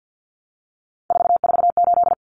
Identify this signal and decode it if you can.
Morse code, heard as a keyed tone, about 50 wpm, 700 Hz. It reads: 448
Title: Common Cut Numbers - 3-Digit 50wpm